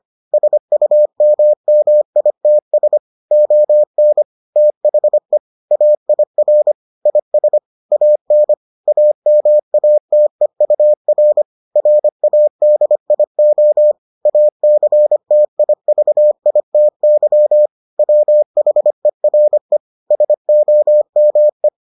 Morse code, heard as a keyed tone, about 25 wpm, 600 Hz. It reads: SUMMITS ON THE AIR IS AN AMATEUR RADIO ACTIVITY WHERE SOME